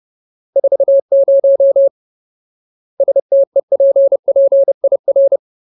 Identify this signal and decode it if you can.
Morse code, heard as a keyed tone, about 30 wpm, 550 Hz. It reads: 40 STEPPIR
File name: Ultimate Word List / 30WPM Ultimate